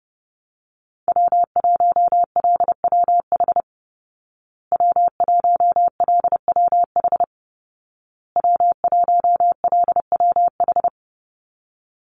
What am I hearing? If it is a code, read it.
Morse code, heard as a keyed tone, about 30 wpm, 700 Hz. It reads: W1LWH W1LWH W1LWH
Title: Call Signs Repeated 2x - 30wpm